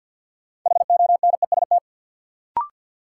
Morse code, heard as a keyed tone, about 47 wpm, 700 Hz. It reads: HONEST
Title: Sets of 5 Words Encore - New Word Drill - 50wpm